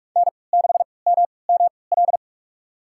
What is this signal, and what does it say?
Morse code, heard as a keyed tone, about 43 wpm, spaced as for 25 wpm, 700 Hz. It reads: N6MML
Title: USA Call Signs - 5 Characters - 3x Character Spacing - 45wpm